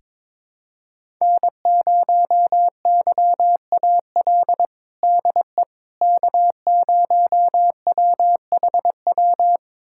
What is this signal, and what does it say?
Morse code, heard as a keyed tone, about 22 wpm, 700 Hz. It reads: N0YAL DE K0WHW
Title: USA Callsign DE Callsign - 5 Characters - 22wpm